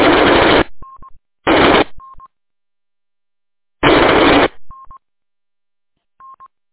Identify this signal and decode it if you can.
There is no code.